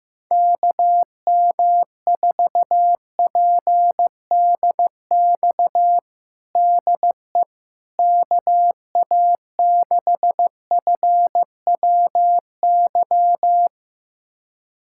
Morse code, heard as a keyed tone, about 15 wpm, 700 Hz. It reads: KM4PDX DE KA6FWY
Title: USA Callsign DE Callsign - 6 Characters - 15wpm